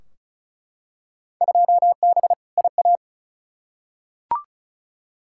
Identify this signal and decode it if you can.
Morse code, heard as a keyed tone, about 34 wpm, 700 Hz. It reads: 2B IA
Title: Field Day Exchanges Encore - 35wpm